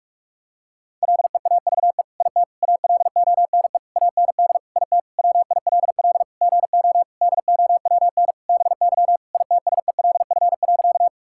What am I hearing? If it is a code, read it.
Morse code, heard as a keyed tone, about 47 wpm, 700 Hz. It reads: LEAVE IT ALONE AND IT WILL GO DOWN BY ITSELF.
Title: Sentences from Top 1000 Words - Review Part 5 - 45wpm